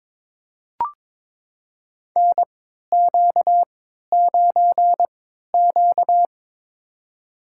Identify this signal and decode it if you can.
Morse code, heard as a keyed tone, about 22 wpm, spaced as for 13 wpm, 700 Hz. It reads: NQ9Q
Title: USA Call Signs - 4 Characters - 3x Character Spacing - 22wpm